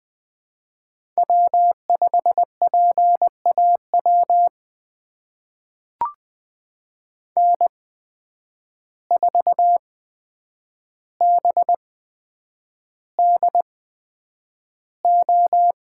Morse code, heard as a keyed tone, about 20 wpm, spaced as for 9 wpm, 700 Hz. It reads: W5PAW N4BDO
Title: USA Call Signs - 5 Characters - 8x Character Spacing - 20wpm